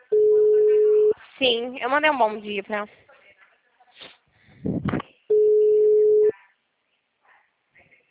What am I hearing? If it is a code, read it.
Ringback tone (Europe, 425 Hz), with speech.